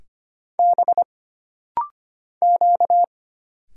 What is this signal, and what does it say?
Morse code, heard as a keyed tone, about 25 wpm, 700 Hz. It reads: B Q
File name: Single Character (Letter:Number:Punctuation:Prosign) - Rapid-Fire - 25wpm